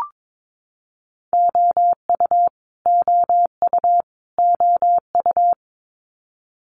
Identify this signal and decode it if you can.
Morse code, heard as a keyed tone, about 22 wpm, 700 Hz. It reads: OU OU OU
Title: Multi-letter Phonemes Repeated 3x 22wpm